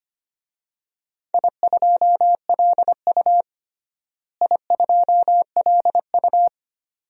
Morse code, heard as a keyed tone, about 25 wpm, 700 Hz. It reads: I2LU I2LU
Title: DXCC-100 Call Signs - 4 Characters - Repeated 2x - 25wpm